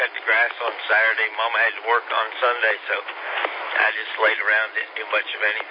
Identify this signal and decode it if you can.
Emergency Alert System (SAME) tones, heard as harsh buzzing bursts, with speech. It is heard from 0.8 to 4.7 s.